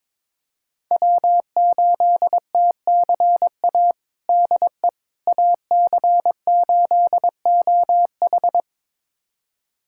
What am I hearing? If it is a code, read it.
Morse code, heard as a keyed tone, about 22 wpm, 700 Hz. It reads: W8TCA DE AC8OH
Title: USA Callsign DE Callsign - 5 Characters - 22wpm